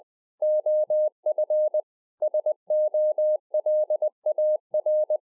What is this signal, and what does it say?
Morse code, heard as a keyed tone, about 20 wpm, 610 Hz. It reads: OF SOLAR